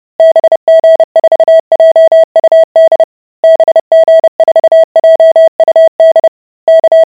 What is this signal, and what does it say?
Morse code, heard as a keyed tone, about 30 wpm, 650 Hz. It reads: BG4JUD BG4JUD K